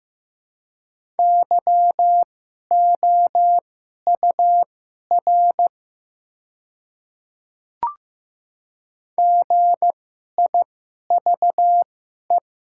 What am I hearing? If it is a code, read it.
Morse code, heard as a keyed tone, about 15 wpm, spaced as for 11 wpm, 700 Hz. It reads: YOUR GIVE
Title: 2x Character Spacing - Top 500 Words - 15wpm